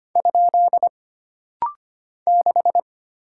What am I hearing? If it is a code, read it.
Morse code, heard as a keyed tone, about 25 wpm, 700 Hz. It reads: ? 6
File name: Single Character (Letter:Number:Punctuation:Prosign) - Rapid-Fire - 25wpm